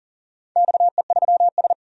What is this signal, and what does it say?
Morse code, heard as a keyed tone, about 40 wpm, 700 Hz. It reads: XE3S
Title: DXCC-100 Call Signs - 4 Characters - 40wpm